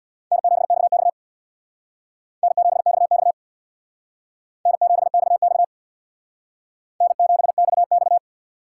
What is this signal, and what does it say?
Morse code, heard as a keyed tone, about 61 wpm, 700 Hz. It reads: N7XX N7XX N7XX N7XX
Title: Speed Racing - USA 1x2 Call Signs 50wpm